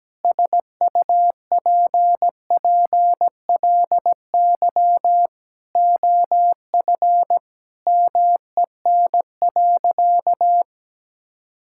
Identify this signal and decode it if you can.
Morse code, heard as a keyed tone, about 17 wpm, 700 Hz. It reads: SUPPLY OF MEN.
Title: Sentences from Top 1000 Words - Review Part 5 - 17wpm